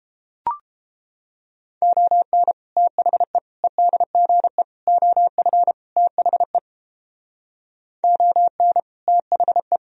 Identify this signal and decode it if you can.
Morse code, heard as a keyed tone, about 32 wpm, 700 Hz. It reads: ON THE EDGE OF THE ON THE